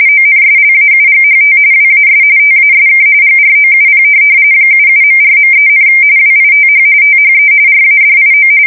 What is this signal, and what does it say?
RTTY, about 45.45 baud. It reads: ONSLAUGHT WAS TASK GROUP 77.4.3, WITH THE RADIO
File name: fsk170 / USS-New-Jersey